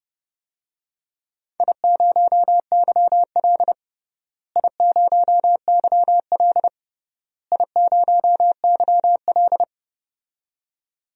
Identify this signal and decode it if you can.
Morse code, heard as a keyed tone, about 30 wpm, 700 Hz. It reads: I0YL I0YL I0YL
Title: DXCC-100 Call Signs - 4 Characters - Repeated 3x - 30wpm